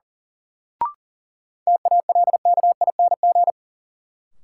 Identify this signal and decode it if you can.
Morse code, heard as a keyed tone, about 37 wpm, 700 Hz. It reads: TALKING
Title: Sets of 2 Words - New Word Drill - 40wpm